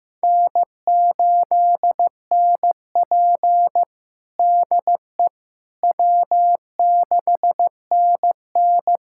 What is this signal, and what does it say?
Morse code, heard as a keyed tone, about 16 wpm, 700 Hz. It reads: N8NP DE W6NN
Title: USA Callsign DE Callsign - 4 Characters - 15wpm